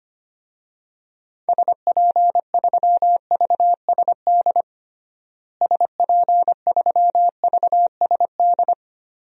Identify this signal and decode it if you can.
Morse code, heard as a keyed tone, about 25 wpm, 700 Hz. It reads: SP3VSD SP3VSD
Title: DXCC-100 Call Signs - 6 Characters - Repeated 3x - 25wpm